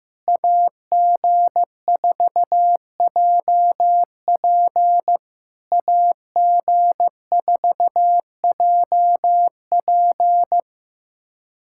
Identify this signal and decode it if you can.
Morse code, heard as a keyed tone, about 15 wpm, 700 Hz. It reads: AG4JP AG4JP
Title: SOTA QSO Practice 15wpm